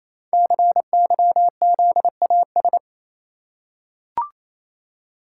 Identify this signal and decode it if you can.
Morse code, heard as a keyed tone, about 29 wpm, 700 Hz. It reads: CYZAS
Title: Code Groups - 5 Letters - 28wpm